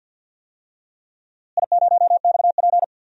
Morse code, heard as a keyed tone, about 51 wpm, 700 Hz. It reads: I0XP
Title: DXCC-100 Call Signs - 4 Characters - Repeated 3x - 50wpm